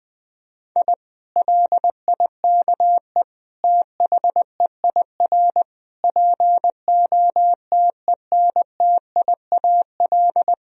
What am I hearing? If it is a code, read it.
Morse code, heard as a keyed tone, about 20 wpm, 700 Hz. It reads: I LIKE THEIR POTENTIAL
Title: Sentences from Top 1000 Words - Review Part 4 - 20wpm